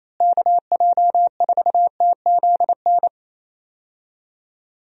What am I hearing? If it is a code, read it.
Morse code, heard as a keyed tone, about 28 wpm, 700 Hz. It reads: KJ4TZN
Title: USA Call Signs - 6 Characters - Repeated 2x - 28wpm